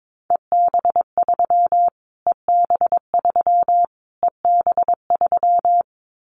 Morse code, heard as a keyed tone, about 22 wpm, 700 Hz. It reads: EB3 EB3 EB3
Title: Callsign Prefixes for Top 25 DXCC - Repeated 3x - 22wpm